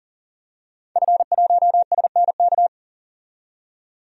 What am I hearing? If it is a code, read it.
Morse code, heard as a keyed tone, about 40 wpm, 700 Hz. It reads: F1SNK
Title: DXCC-100 Call Signs - 5 Characters - 40wpm